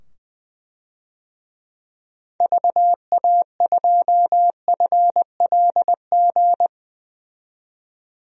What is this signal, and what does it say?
Morse code, heard as a keyed tone, about 20 wpm, 700 Hz. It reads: VA2FLG
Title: DXCC-100 Call Signs - 6 Characters - 20wpm